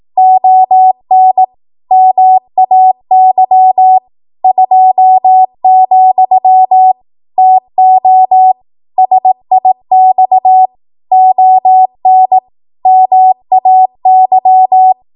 Morse code, heard as a keyed tone, about 18 wpm, 750 Hz. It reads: ON MAY 2, TO SIX ON MAY
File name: Morse / 18WPM